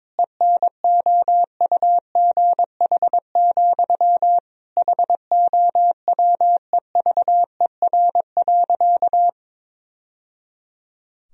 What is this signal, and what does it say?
Morse code, heard as a keyed tone, about 22 wpm, 700 Hz. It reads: ENOUGH, HOWEVER.
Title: Sentences from Top 1000 Words - Review Part 1 - 22wpm